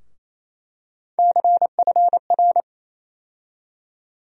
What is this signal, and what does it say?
Morse code, heard as a keyed tone, about 27 wpm, 700 Hz. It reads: CFR
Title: USA Call Sign Postfixes - 3 Letters 28wpm